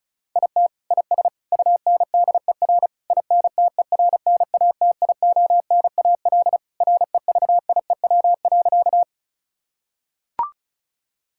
Morse code, heard as a keyed tone, about 36 wpm, 700 Hz. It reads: IT IS UNDER INTERNATIONAL REVIEW.
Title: Sentences from Top 1000 Words - Review Part 6 - 35wpm